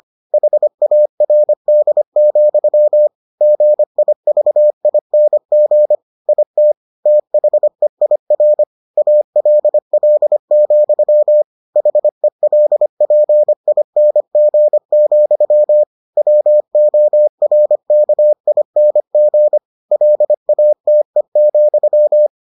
Morse code, heard as a keyed tone, about 25 wpm, 600 Hz. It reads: HARD, GIVING IT THEIR ALL, HELPING, WORKING LATE,